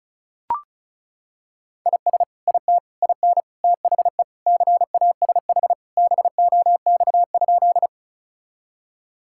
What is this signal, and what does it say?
Morse code, heard as a keyed tone, about 36 wpm, 690 Hz. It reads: IS IT IN THE CASH BOX?
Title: Sentences from Top 1000 Words - Review Part 5 - 35wpm